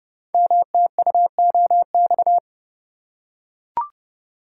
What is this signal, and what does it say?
Morse code, heard as a keyed tone, about 29 wpm, 700 Hz. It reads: MTUOX
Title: Code Groups - 5 Letters - 30wpm